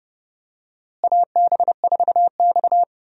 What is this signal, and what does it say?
Morse code, heard as a keyed tone, about 30 wpm, 700 Hz. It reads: AB4X
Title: USA Call Signs - 4 Characters - Repeated 3x - 30wpm